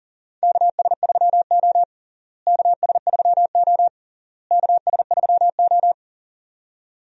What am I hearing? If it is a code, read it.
Morse code, heard as a keyed tone, about 37 wpm, 700 Hz. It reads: KS3O KS3O KS3O
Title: USA Call Signs - 4 Characters - Repeated 3x - 40wpm